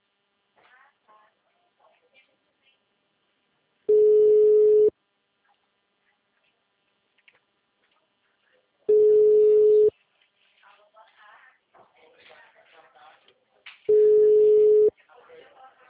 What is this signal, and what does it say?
Ringback tone (Europe, 425 Hz), with speech.
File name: out- -2033- - - .31228